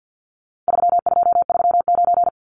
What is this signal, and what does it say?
Morse code, heard as a keyed tone, about 50 wpm, 700 Hz. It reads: 3239
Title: Numbers - 4-Digit 50wpm